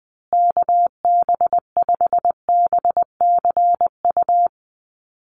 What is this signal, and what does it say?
Morse code, heard as a keyed tone, about 20 wpm, 700 Hz. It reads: KB5BCU
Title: USA Call Signs Repeated 2x 20wpm